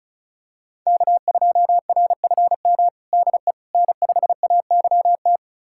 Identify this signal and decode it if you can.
Morse code, heard as a keyed tone, about 34 wpm, 700 Hz. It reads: K2RFM DE N5AYT
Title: USA Callsign DE Callsign - 5 Characters - 35wpm